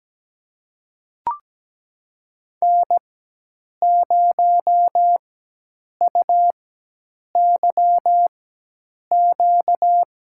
Morse code, heard as a keyed tone, about 17 wpm, spaced as for 8 wpm, 700 Hz. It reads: N0UYQ
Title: USA Call Signs - 5 Characters - 4x Character Spacing - 17wpm